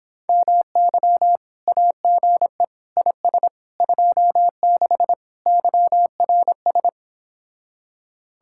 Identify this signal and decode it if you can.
Morse code, heard as a keyed tone, about 26 wpm, 700 Hz. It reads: MY AGE IS 26 YRS